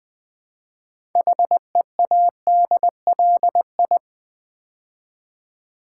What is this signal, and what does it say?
Morse code, heard as a keyed tone, about 20 wpm, 700 Hz. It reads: HEADLI
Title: Build A Word - 2 Letters At A Time - 8-Letter Words - 20wpm